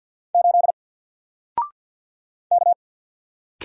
Morse code, heard as a keyed tone, about 48 wpm, 700 Hz. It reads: 8 K
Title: Single Letter-Number Alt 50wpm